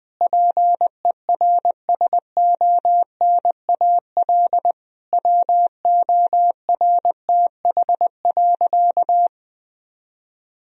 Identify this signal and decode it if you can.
Morse code, heard as a keyed tone, about 20 wpm, 700 Hz. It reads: PERSONAL WORTH.
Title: Sentences from Top 1000 Words - Review Part 4 - 20wpm